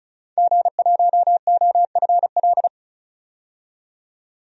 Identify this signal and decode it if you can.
Morse code, heard as a keyed tone, about 34 wpm, 700 Hz. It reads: G1OFL
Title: DXCC-100 Call Signs - 5 Characters - 35wpm